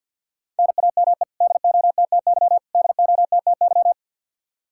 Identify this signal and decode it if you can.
Morse code, heard as a keyed tone, about 48 wpm, 700 Hz. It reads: NAME DOTTY DOTTY
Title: Practice QSO - Exchange 1 - Beginning - 50wpm